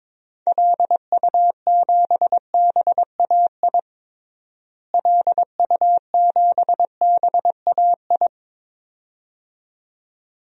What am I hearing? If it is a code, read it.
Morse code, heard as a keyed tone, about 22 wpm, 700 Hz. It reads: LU7BAI LU7BAI
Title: DXCC-100 Call Signs - 6 Characters - Repeated 2x - 22wpm